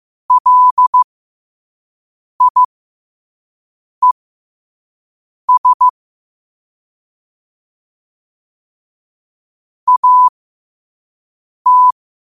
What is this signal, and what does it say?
Morse code, heard as a keyed tone, about 15 wpm, spaced as for 5 wpm, 1000 Hz. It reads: LIES AT
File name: qotdfiles / 5wpm